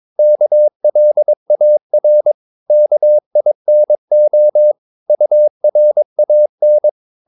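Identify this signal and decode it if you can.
Morse code, heard as a keyed tone, about 22 wpm, 600 Hz. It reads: KLAR KINO URAN
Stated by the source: Morse practice of short Danish words at 22 wpm, from CW med Gnister.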